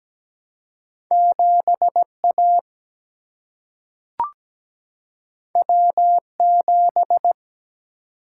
Morse code, heard as a keyed tone, about 17 wpm, 700 Hz. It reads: 7A W7
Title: Single-Letter Single-Number 17wpm